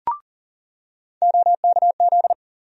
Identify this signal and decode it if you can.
Morse code, heard as a keyed tone, about 38 wpm, 700 Hz. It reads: OKZ
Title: USA Call Sign Postfixes - 3 Letters 40wpm